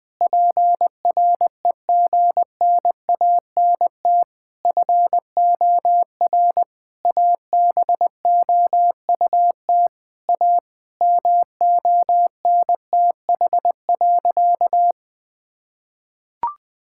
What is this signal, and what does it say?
Morse code, heard as a keyed tone, about 20 wpm, 700 Hz. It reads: PREGNANT FOR ABOUT A MONTH.
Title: Sentences from Top 1000 Words - Review Part 6 - 20wpm